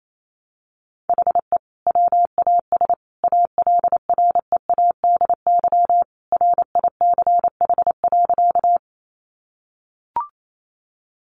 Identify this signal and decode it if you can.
Morse code, heard as a keyed tone, about 28 wpm, 700 Hz. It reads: HE WAS ALREADY RICH.
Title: Sentences from Top 1100 Words 28wpm